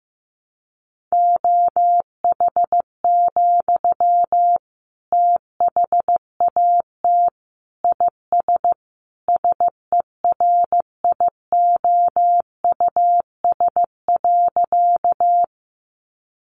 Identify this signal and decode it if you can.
Morse code, heard as a keyed tone, about 15 wpm, 700 Hz. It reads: OH, THAT IS SERIOUS.
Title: Sentences from Top 1100 Words 15wpm